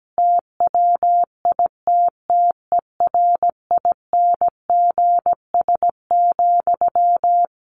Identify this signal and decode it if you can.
Morse code, heard as a keyed tone, about 17 wpm, 700 Hz. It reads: TWITTERINGS,
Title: A Sound Of Thunder 17wpm